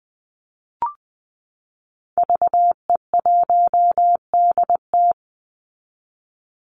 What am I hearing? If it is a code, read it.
Morse code, heard as a keyed tone, about 20 wpm, 700 Hz. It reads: VE1DT
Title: Call Signs Repeated 3x - 20wpm